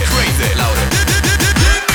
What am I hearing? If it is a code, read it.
There is no code.